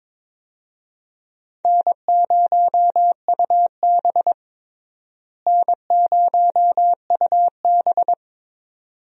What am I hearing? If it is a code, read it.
Morse code, heard as a keyed tone, about 22 wpm, 700 Hz. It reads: N0UB N0UB
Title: USA Call Signs - 4 Characters - Repeated 2x - 22wpm